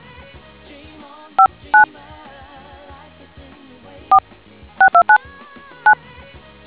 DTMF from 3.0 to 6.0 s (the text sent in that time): 46299